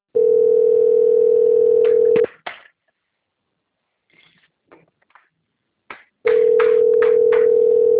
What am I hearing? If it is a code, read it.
Ringback tone (North America), 440 + 480 Hz.